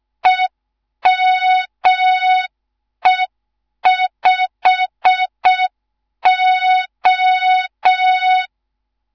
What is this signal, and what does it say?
Morse code, heard as a keyed tone, about 6 wpm, 3700 Hz. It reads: EME5O
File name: Morze-HA2MN / hang / 11 GYAKORLAT